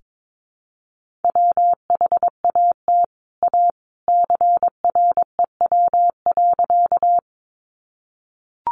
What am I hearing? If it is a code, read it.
Morse code, heard as a keyed tone, about 22 wpm, 700 Hz. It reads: WHAT A CREW.
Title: Sentences from Top 1500 Words 22wpm